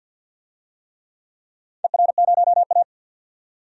Morse code, heard as a keyed tone, about 51 wpm, 700 Hz. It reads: ER0A
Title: DXCC-100 Call Signs - 4 Characters - 50wpm